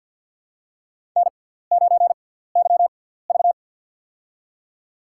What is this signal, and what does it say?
Morse code, heard as a keyed tone, about 52 wpm, spaced as for 17 wpm, 700 Hz. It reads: N9YV
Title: USA Call Signs - 4 Characters - 6x Character Spacing - 50wpm